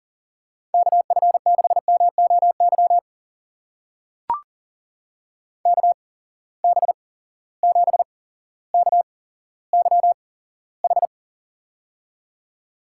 Morse code, heard as a keyed tone, about 40 wpm, spaced as for 17 wpm, 700 Hz. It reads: KF6MOY KB7KYH
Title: USA Call Signs - 6 Characters - 8x Character Spacing - 40wpm